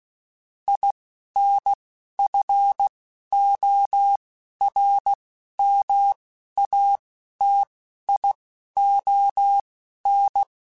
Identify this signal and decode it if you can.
Morse code, heard as a keyed tone, about 16 wpm, spaced as for 11 wpm, 800 Hz. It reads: INFORMATION
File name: Technical Subjects / Technical Subjects-mp3s / Sw Pwr Sup @10